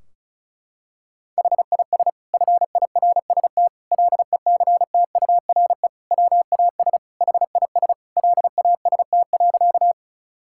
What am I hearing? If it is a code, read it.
Morse code, heard as a keyed tone, about 36 wpm, 700 Hz. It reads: HIS FIRST LECTURE WAS HIS LAST.
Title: Sentences from Top 1600 Words 35wpm